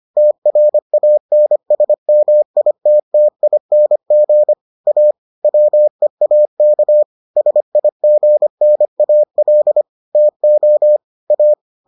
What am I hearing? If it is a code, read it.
Morse code, heard as a keyed tone, about 25 wpm, 600 Hz. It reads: TRANSMITTING A WEAK SIGNAL TO A